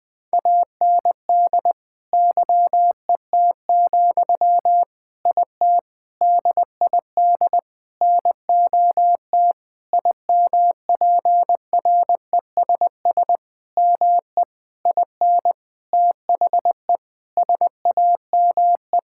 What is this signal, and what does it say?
Morse code, heard as a keyed tone, about 20 wpm, 700 Hz. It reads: AND YET, IT DID NOT IMPRESS ME IN THE SAME
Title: Sentences from Top 1600 Words 20wpm